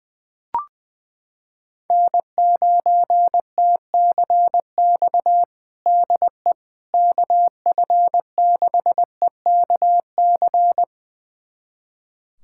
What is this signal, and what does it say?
Morse code, heard as a keyed tone, about 20 wpm, 700 Hz. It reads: N9TCX DE KF6EKC
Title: USA Callsign DE Callsign 20wpm